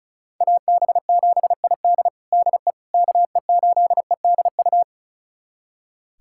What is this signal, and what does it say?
Morse code, heard as a keyed tone, about 34 wpm, 700 Hz. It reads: AB7ID DE KE8EDU
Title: USA Callsign DE Callsign 35wpm